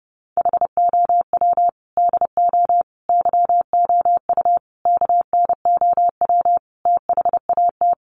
Morse code, heard as a keyed tone, about 30 wpm, 700 Hz. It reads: HOW DO YOU KNOW THAT
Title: Sets of 5 Words Encore 30wpm